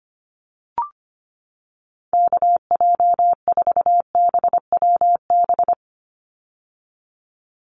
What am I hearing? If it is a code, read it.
Morse code, heard as a keyed tone, about 25 wpm, 700 Hz. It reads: KJ4BWB
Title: USA Call Signs 25wpm